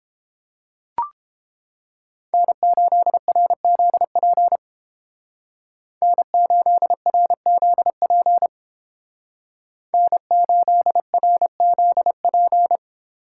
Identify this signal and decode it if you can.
Morse code, heard as a keyed tone, about 31 wpm, 700 Hz. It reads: N8RZP N8RZP N8RZP